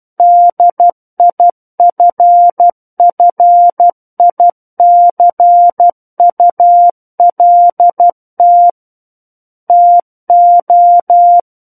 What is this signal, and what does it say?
Morse code, heard as a keyed tone, about 12 wpm, 700 Hz. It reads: DIFFICULT TO
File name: Book One / 03-Chapter 3 - 12 WPM